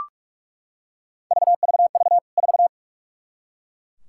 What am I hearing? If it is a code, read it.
Morse code, heard as a keyed tone, about 41 wpm, 700 Hz. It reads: VVV 4
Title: K1USN SST QSO Practice 45wpm